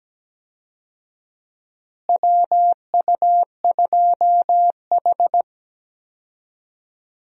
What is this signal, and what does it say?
Morse code, heard as a keyed tone, about 17 wpm, 700 Hz. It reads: WU2H
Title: USA Call Signs - 4 Characters - Repeated 3x - 17wpm